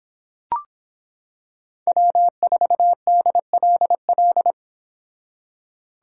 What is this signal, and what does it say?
Morse code, heard as a keyed tone, about 25 wpm, 700 Hz. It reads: W4DLL